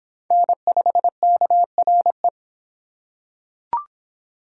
Morse code, heard as a keyed tone, about 25 wpm, 700 Hz. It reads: N5KRE